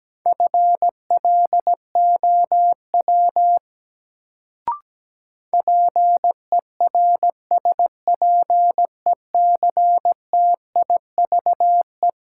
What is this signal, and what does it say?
Morse code, heard as a keyed tone, about 17 wpm, 700 Hz. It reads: FLOW PERSPECTIVE
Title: Sentences from Top 1200 Words - New Word Drill Alt - 17wpm